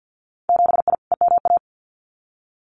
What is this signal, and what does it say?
Morse code, heard as a keyed tone, about 50 wpm, 700 Hz. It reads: 7I EMA
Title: Winter Field Day Exchanges Encore 50wpm